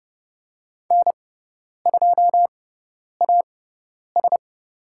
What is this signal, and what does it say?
Morse code, heard as a keyed tone, about 30 wpm, spaced as for 10 wpm, 700 Hz. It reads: N2AS
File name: Lesson 17g - Letter u - Call Signs 30wpm